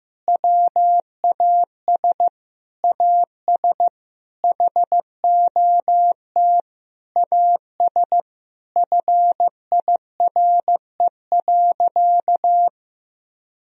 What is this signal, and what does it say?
Morse code, heard as a keyed tone, about 15 wpm, 700 Hz. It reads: WAS AS HOT AS FIRE.
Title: Sentences from Top 1000 Words - Review Part 3 - 15wpm